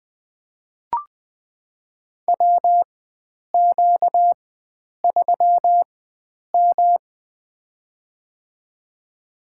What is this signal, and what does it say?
Morse code, heard as a keyed tone, about 20 wpm, spaced as for 9 wpm, 700 Hz. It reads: WQ3M
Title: USA Call Signs - 4 Characters - 4x Character Spacing - 20wpm